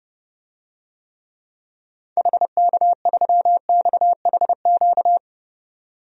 Morse code, heard as a keyed tone, about 32 wpm, 700 Hz. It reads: HK3XHQ
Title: DXCC-100 Call Signs - 6 Characters - 30wpm